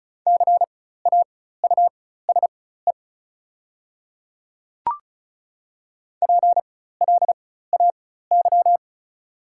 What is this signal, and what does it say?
Morse code, heard as a keyed tone, about 36 wpm, spaced as for 16 wpm, 700 Hz. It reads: CAUSE PLAY
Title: 4x Character Spacing - Top 500 Words - 35wpm